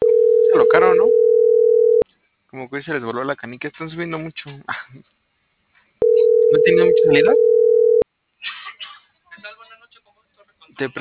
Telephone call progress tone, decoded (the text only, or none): ringback tone